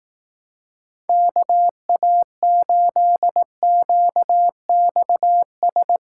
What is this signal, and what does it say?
Morse code, heard as a keyed tone, about 18 wpm, 700 Hz. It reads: KA8QXS
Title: Speed Racing - USA 2x3 Call Signs 15wpm